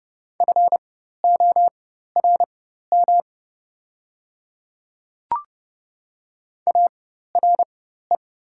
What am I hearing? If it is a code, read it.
Morse code, heard as a keyed tone, about 30 wpm, spaced as for 14 wpm, 700 Hz. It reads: FORM ARE
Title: 4x Character Spacing - Top 200 Words - 30wpm